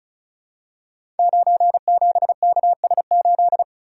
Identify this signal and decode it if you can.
Morse code, heard as a keyed tone, about 36 wpm, 700 Hz. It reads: 97KS8
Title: Code Groups - 5 Letter:Numbers - 35wpm